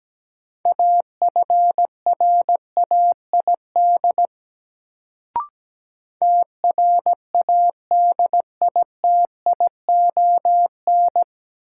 Morse code, heard as a keyed tone, about 17 wpm, 700 Hz. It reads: AFRAID TRADITION
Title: Sentences from Top 1500 Words - New Word Drill Alt - 17wpm